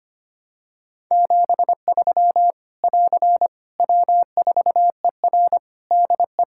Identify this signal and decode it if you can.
Morse code, heard as a keyed tone, about 25 wpm, 700 Hz. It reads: 73 <AR> W4ER DE
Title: K1USN SST QSO Practice 25wpm